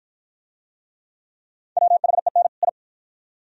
Morse code, heard as a keyed tone, about 53 wpm, 700 Hz. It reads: WHEN I